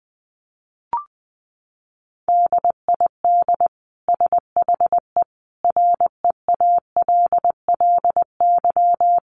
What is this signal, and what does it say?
Morse code, heard as a keyed tone, about 20 wpm, 700 Hz. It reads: DID SHE REALLY
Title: Sentences from Top 1400 Words 20wpm